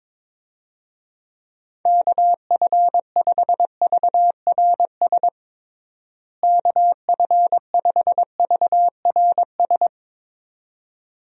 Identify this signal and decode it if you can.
Morse code, heard as a keyed tone, about 22 wpm, 700 Hz. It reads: KF5VRS KF5VRS
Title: USA Call Signs - 6 Characters - Repeated 2x - 22wpm